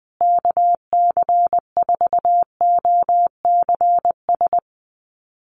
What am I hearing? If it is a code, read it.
Morse code, heard as a keyed tone, about 20 wpm, 700 Hz. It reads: KC4OCS
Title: USA Call Signs Repeated 2x 20wpm